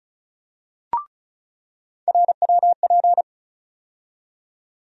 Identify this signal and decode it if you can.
Morse code, heard as a keyed tone, about 36 wpm, 700 Hz. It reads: RWP
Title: USA Call Sign Postfixes - 3 Letters 35wpm